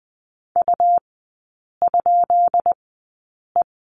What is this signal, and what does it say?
Morse code, heard as a keyed tone, about 20 wpm, spaced as for 8 wpm, 700 Hz. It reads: U?E
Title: 172 Lesson 28e - Number 9 - Three Characters 20wpm